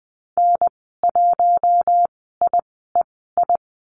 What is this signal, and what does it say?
Morse code, heard as a keyed tone, about 20 wpm, spaced as for 14 wpm, 700 Hz. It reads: N1IEI
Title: 042 Lesson 8g - Number 1 - Call Signs 20wpm